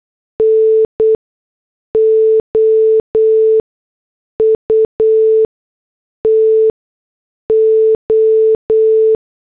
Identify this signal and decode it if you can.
Morse code, heard as a keyed tone, about 8 wpm, spaced as for 6 wpm, 440 Hz. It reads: N OUT O